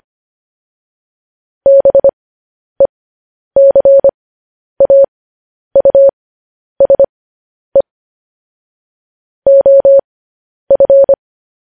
Morse code, heard as a keyed tone, about 25 wpm, spaced as for 10 wpm, 550 Hz. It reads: BECAUSE OF
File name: podcast / News headlines-1x-25x10